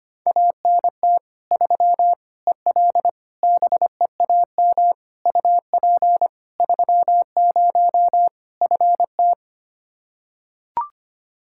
Morse code, heard as a keyed tone, about 25 wpm, 700 Hz. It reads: ANT 3 EL BEAM UP 30 FT
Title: Practice QSO - Exchange 2 - Info - 25wpm